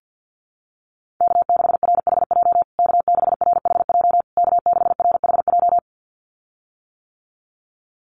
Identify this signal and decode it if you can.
Morse code, heard as a keyed tone, about 50 wpm, 700 Hz. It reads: K6RHJ K6RHJ K6RHJ
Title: USA Call Signs Repeated 3x 50wpm